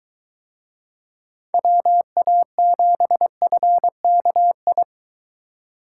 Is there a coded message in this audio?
Yes: Morse code from 1.5 to 4.8 s, heard as a keyed tone, about 23 wpm, 700 Hz. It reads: WA7FKI